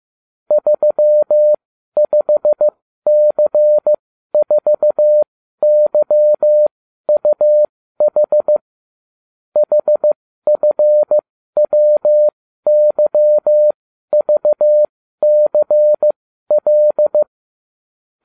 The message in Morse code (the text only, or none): 35C4YUH HFWYVCL